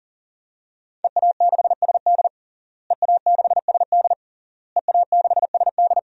Morse code, heard as a keyed tone, about 40 wpm, 700 Hz. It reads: EA6SD EA6SD EA6SD
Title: DXCC-100 Call Signs - 5 Characters - Repeated 3x - 40wpm